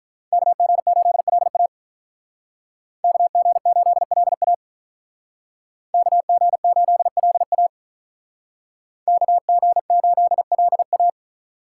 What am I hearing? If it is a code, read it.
Morse code, heard as a keyed tone, about 45 wpm, 700 Hz. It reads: KG8LA KG8LA KG8LA KG8LA